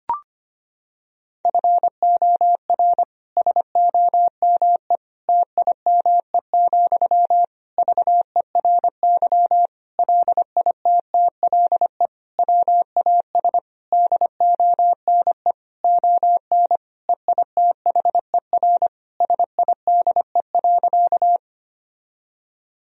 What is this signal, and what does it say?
Morse code, heard as a keyed tone, about 25 wpm, 700 Hz. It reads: FOR SOME TIME, VERY LITTLE WAS DONE ON EITHER SIDE.
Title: Sentences from Top 1000 Words - Review Part 4 - 25wpm